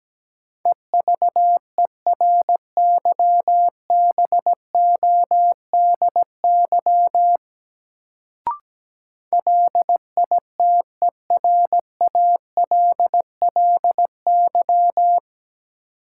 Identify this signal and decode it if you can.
Morse code, heard as a keyed tone, about 17 wpm, 700 Hz. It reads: EVERYBODY LITERALLY
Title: Sentences from Top 1400 Words - New Word Drill Alt - 17wpm